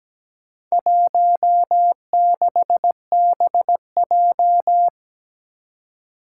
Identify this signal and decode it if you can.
Morse code, heard as a keyed tone, about 17 wpm, 700 Hz. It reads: 16BJ
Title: Four Challenging Letter-Number Combinations 17wpm